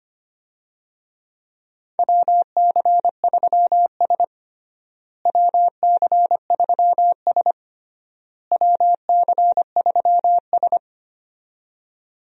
Morse code, heard as a keyed tone, about 25 wpm, 700 Hz. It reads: WC3S WC3S WC3S
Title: USA Call Signs - 4 Characters - Repeated 3x - 25wpm